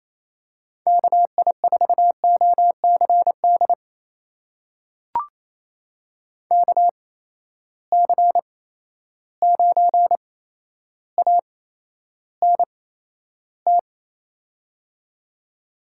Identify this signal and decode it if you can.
Morse code, heard as a keyed tone, about 28 wpm, spaced as for 12 wpm, 700 Hz. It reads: KI4OCD KC9ANT
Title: USA Call Signs - 6 Characters - 8x Character Spacing - 28wpm